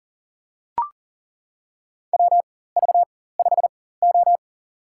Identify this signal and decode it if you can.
Morse code, heard as a keyed tone, about 40 wpm, spaced as for 19 wpm, 700 Hz. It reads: WV5O
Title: USA Call Signs - 4 Characters - 4x Character Spacing - 40wpm